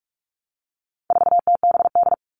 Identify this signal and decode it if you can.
Morse code, heard as a keyed tone, about 44 wpm, 700 Hz. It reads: 4TBD
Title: All Cut Numbers - 4-Digit 45wpm